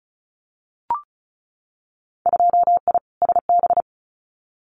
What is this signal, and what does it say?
Morse code, heard as a keyed tone, about 34 wpm, 700 Hz. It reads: 2I SB
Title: Winter Field Day Exchanges Encore 35wpm